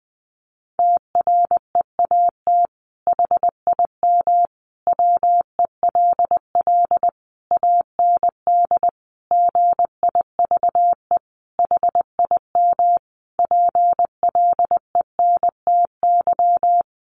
Morse code, heard as a keyed tone, about 20 wpm, 700 Hz. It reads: TREAT HIM WELL AND GIVE HIM PLENTY
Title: Sentences from Top 1300 Words 20wpm